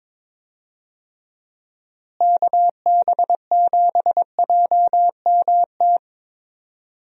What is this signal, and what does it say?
Morse code, heard as a keyed tone, about 22 wpm, 700 Hz. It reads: KB7JMT
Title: USA Call Signs - 6 Characters - Repeated 3x - 22wpm